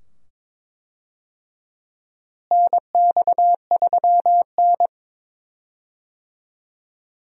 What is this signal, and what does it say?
Morse code, heard as a keyed tone, about 22 wpm, 700 Hz. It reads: NX3N
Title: USA Call Signs - 4 Characters - Repeated 2x - 22wpm